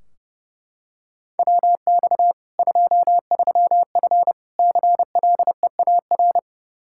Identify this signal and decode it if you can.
Morse code, heard as a keyed tone, about 30 wpm, 700 Hz. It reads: WX 23F CLEAR
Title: K1USN SST QSO Practice 30wpm